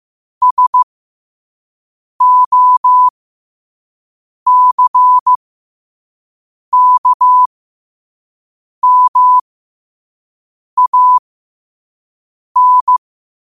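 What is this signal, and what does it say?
Morse code, heard as a keyed tone, about 15 wpm, spaced as for 5 wpm, 1000 Hz. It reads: SOCKMAN
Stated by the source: Morse code sent at 5 wpm.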